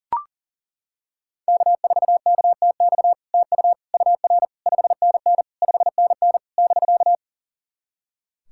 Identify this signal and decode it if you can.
Morse code, heard as a keyed tone, about 40 wpm, 700 Hz. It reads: K4KTX TU UR 5NN 5NN <BK>
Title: SOTA QSO Practice 40wpm